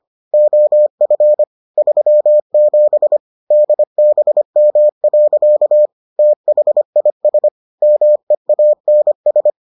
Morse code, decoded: OF 37 DBM. THIS MEANS

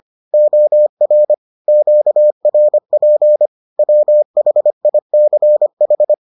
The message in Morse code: OR QRP WHICH

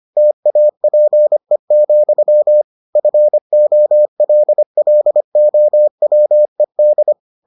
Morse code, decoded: TAPE, FOLLOWED

